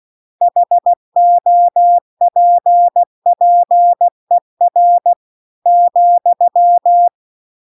Morse code, heard as a keyed tone, about 16 wpm, 700 Hz. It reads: HOPPER ,